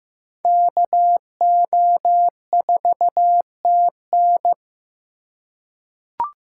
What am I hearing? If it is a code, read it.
Morse code, heard as a keyed tone, about 15 wpm, 700 Hz. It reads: KO4TN
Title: USA Call Signs - 5 Characters - 4x Character Spacing - 15wpm